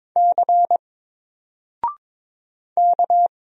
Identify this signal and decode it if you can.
Morse code, heard as a keyed tone, about 22 wpm, 700 Hz. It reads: C K
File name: Single-Letter-Number-22wpm